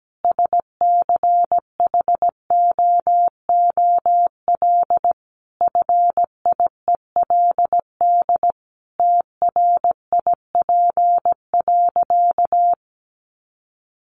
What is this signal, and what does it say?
Morse code, heard as a keyed tone, about 17 wpm, 700 Hz. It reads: SCHOOL FIELD TRIP.
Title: Additional Easy Phrases 17wpm